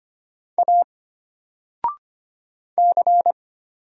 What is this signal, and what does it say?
Morse code, heard as a keyed tone, about 25 wpm, 700 Hz. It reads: A C
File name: Single-Letter-Number-25wpm